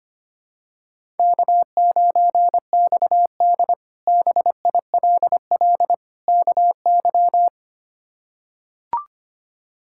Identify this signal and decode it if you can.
Morse code, heard as a keyed tone, about 25 wpm, 700 Hz. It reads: K9XD BILL KY
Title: K1USN SST QSO Practice - 25wpm